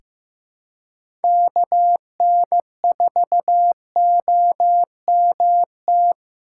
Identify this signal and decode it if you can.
Morse code, heard as a keyed tone, about 15 wpm, 700 Hz. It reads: KN4OMT
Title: USA Call Signs - 6 Characters - 4x Character Spacing - 15wpm